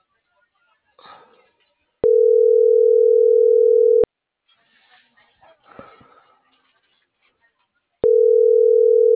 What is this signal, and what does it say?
Ringback tone (North America), 440 + 480 Hz.